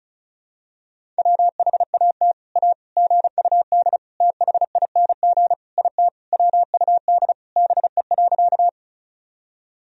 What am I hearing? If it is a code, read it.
Morse code, heard as a keyed tone, about 36 wpm, 700 Hz. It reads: WHAT A GUD THING IT WUD BE.
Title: Sentences from Top 200 Words and Abbreviations 35wpm